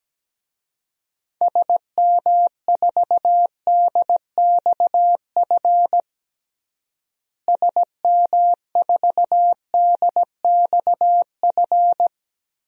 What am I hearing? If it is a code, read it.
Morse code, heard as a keyed tone, about 17 wpm, 700 Hz. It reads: SM4DXF SM4DXF
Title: DXCC-100 Call Signs - 6 Characters - Repeated 2x - 17wpm